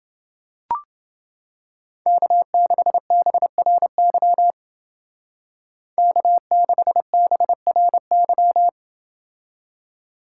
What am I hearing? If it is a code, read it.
Morse code, heard as a keyed tone, about 29 wpm, 700 Hz. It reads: K6BRY K6BRY